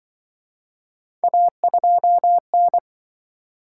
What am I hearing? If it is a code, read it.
Morse code, heard as a keyed tone, about 28 wpm, 700 Hz. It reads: A2N